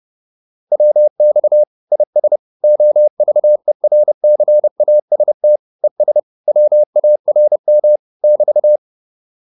Morse code, heard as a keyed tone, about 30 wpm, 600 Hz. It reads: WX IS OVERCAST ES WARM <BT>
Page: advanced-practice-files